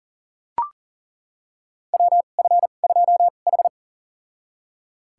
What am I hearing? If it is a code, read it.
Morse code, heard as a keyed tone, about 40 wpm, spaced as for 29 wpm, 700 Hz. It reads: WF2H
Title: USA Call Signs - 4 Characters - 2x Character Spacing - 40wpm